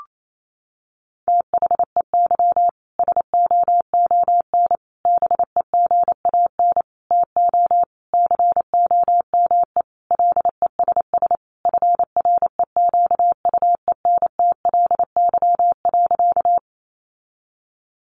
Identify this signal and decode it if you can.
Morse code, heard as a keyed tone, about 28 wpm, 700 Hz. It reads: THEY SOON BEGAN TO COME LESS FREQUENTLY.
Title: Sentences from Top 1300 Words 28wpm